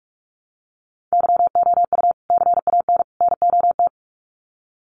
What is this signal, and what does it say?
Morse code, heard as a keyed tone, about 44 wpm, 700 Hz. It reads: YOU CAN NOT
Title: Sets of 3 Words 45wpm